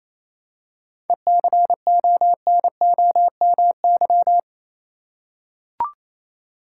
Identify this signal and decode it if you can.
Morse code, heard as a keyed tone, about 28 wpm, 700 Hz. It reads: ECONOMY
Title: Seven Letter Words - 28wpm